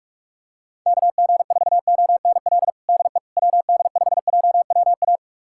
Morse code, heard as a keyed tone, about 46 wpm, 700 Hz. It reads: KG4ONL DE WD5JWA
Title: USA Callsign DE Callsign - 6 Characters - 45wpm